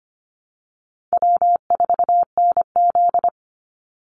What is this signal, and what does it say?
Morse code, heard as a keyed tone, about 25 wpm, 700 Hz. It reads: W4NZ
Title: Easy Call Signs - 25wpm